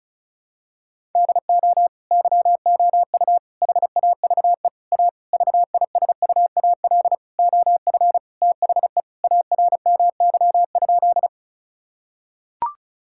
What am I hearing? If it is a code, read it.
Morse code, heard as a keyed tone, about 36 wpm, 700 Hz. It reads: DO YOU HAVE A VISUAL OF THE ARMY?
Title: Sentences from Top 1000 Words - Review Part 6 - 35wpm